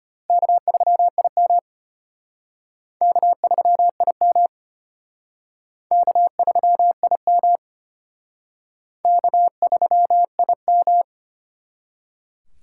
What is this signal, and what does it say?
Morse code, heard as a keyed tone, about 32 wpm, 700 Hz. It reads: K3IM K3IM K3IM K3IM